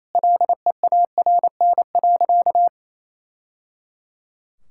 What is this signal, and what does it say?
Morse code, heard as a keyed tone, about 28 wpm, 700 Hz. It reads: LEARN.
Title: Sentences from Top 1000 Words - Review Part 6 - 28wpm